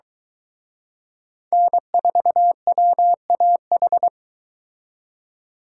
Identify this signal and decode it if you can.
Morse code, heard as a keyed tone, about 23 wpm, 700 Hz. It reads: N4WAH